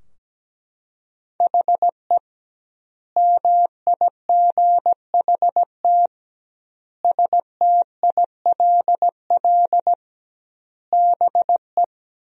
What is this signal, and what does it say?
Morse code, heard as a keyed tone, about 17 wpm, 700 Hz. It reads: HE MIGHT STILL BE
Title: 2x Word-Spacing - Sentences from Top 300 Words - 17wpm